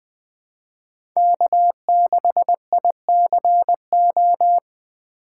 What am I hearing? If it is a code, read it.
Morse code, heard as a keyed tone, about 20 wpm, 700 Hz. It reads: K6ICO
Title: USA Call Signs - 5 Characters - 4x Character Spacing - 20wpm